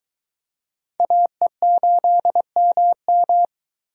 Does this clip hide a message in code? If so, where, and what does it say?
Yes: Morse code from 1.0 to 3.4 s, heard as a keyed tone, about 21 wpm, 700 Hz. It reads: AE8MM